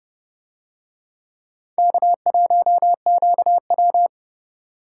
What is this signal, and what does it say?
Morse code, heard as a keyed tone, about 28 wpm, 700 Hz. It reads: K1QW